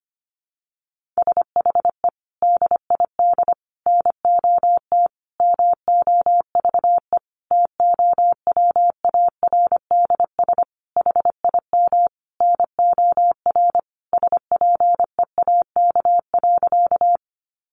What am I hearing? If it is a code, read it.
Morse code, heard as a keyed tone, about 25 wpm, 700 Hz. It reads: SHE DID NOT MOVE TOWARDS HIM NOR SPEAK.
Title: Sentences from Top 1500 Words 25wpm